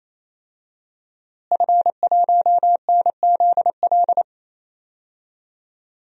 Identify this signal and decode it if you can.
Morse code, heard as a keyed tone, about 27 wpm, 700 Hz. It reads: F1NZL
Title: DXCC-100 Call Signs - 5 Characters - Repeated 3x - 28wpm